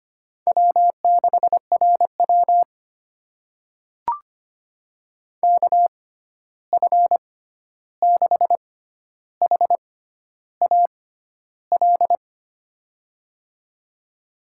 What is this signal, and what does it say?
Morse code, heard as a keyed tone, about 25 wpm, spaced as for 11 wpm, 700 Hz. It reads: W6RW KF6HAL
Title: USA Call Signs - 6x Character Spacing - 25wpm